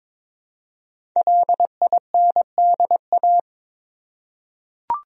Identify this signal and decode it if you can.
Morse code, heard as a keyed tone, about 22 wpm, 700 Hz. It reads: LINDA
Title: USA Common Names - 22wpm